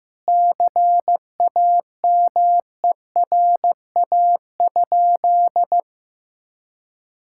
Morse code, heard as a keyed tone, about 15 wpm, 700 Hz. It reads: CAMERA?
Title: Sentences from Top 1000 Words - Review Part 5 - 15wpm